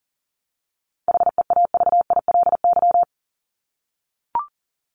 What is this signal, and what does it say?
Morse code, heard as a keyed tone, about 40 wpm, 700 Hz. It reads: HEAVILY
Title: LY Suffix - Top 100 Words with Base Words 40wpm